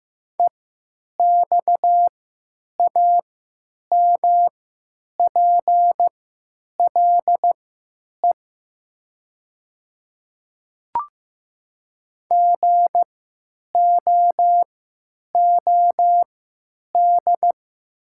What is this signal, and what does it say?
Morse code, heard as a keyed tone, about 15 wpm, spaced as for 9 wpm, 700 Hz. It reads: EXAMPLE GOOD
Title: 3x Character Spacing - Top 300 Words - 15wpm